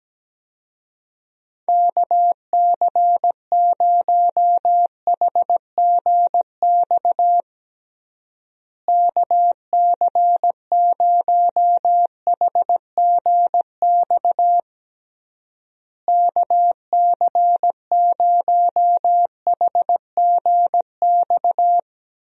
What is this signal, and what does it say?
Morse code, heard as a keyed tone, about 17 wpm, 700 Hz. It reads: KC0HGX KC0HGX KC0HGX
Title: USA Call Signs - 6 Characters - Repeated 3x - 17wpm